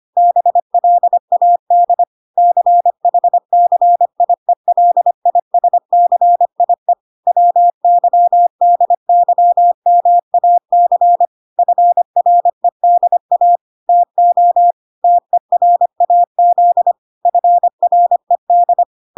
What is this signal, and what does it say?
Morse code, heard as a keyed tone, about 25 wpm, 700 Hz. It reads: BLAD CHCIELISCIE WYDYMAC FREDA TO TERAZ FRED